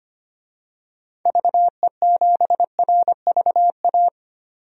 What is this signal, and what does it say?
Morse code, heard as a keyed tone, about 25 wpm, 700 Hz. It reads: VE7RVA
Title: DXCC-100 Call Signs - 6 Characters - 25wpm